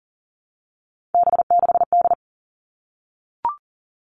Morse code, heard as a keyed tone, about 40 wpm, 700 Hz. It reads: B6D
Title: All Cut Numbers - 3-Digit 40wpm